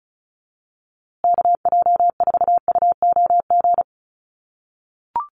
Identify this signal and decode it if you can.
Morse code, heard as a keyed tone, about 34 wpm, 700 Hz. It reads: KJ4UOG
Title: USA Call Signs Repeated 3x 35wpm